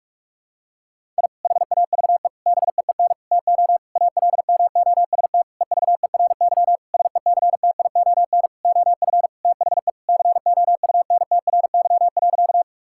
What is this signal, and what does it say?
Morse code, heard as a keyed tone, about 41 wpm, 700 Hz. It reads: I HAVE BEEN TO ALMOST EVERY SECTION OF THE COUNTRY.
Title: Sentences from Top 1000 Words - Review Part 5 - 45wpm